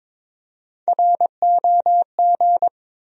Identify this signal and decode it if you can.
Morse code, heard as a keyed tone, about 22 wpm, 700 Hz. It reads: ROG
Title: USA Common Names - Max 3 Letters - 22wpm